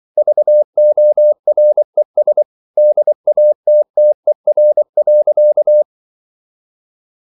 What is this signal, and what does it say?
Morse code, heard as a keyed tone, about 24 wpm, 600 Hz. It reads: VORES DATTER.
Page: Ferie 24 wpm